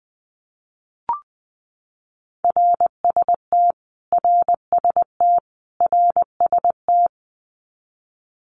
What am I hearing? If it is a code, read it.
Morse code, heard as a keyed tone, about 20 wpm, 700 Hz. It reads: RST RST RST
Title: Abbreviations Repeated 3x - 20wpm